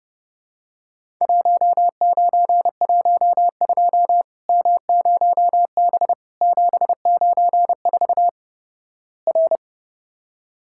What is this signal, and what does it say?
Morse code, heard as a keyed tone, about 32 wpm, 700 Hz. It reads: 1912 M0B 794 R
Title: Worked All Europe DX Contest - QTC Exchange - 30wpm